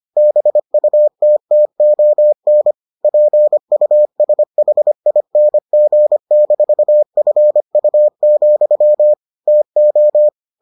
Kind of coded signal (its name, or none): Morse code